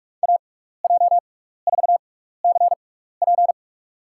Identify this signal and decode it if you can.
Morse code, heard as a keyed tone, about 46 wpm, spaced as for 15 wpm, 700 Hz. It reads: AJ4CP
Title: USA Call Signs - 6x Character Spacing - 45wpm